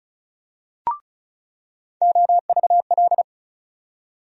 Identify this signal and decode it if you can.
Morse code, heard as a keyed tone, about 34 wpm, 700 Hz. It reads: OVL
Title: USA Call Sign Postfixes - 3 Letters 35wpm